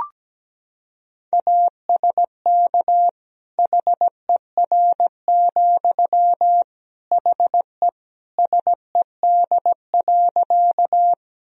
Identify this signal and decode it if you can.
Morse code, heard as a keyed tone, about 17 wpm, 700 Hz. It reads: ASK HER, HE SED.
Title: Sentences from Top 200 Words and Abbreviations 17wpm